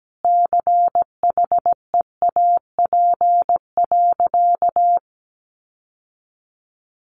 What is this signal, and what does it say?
Morse code, heard as a keyed tone, about 17 wpm, 700 Hz. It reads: CHEAP.
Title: Sentences from Top 1200 Words 17wpm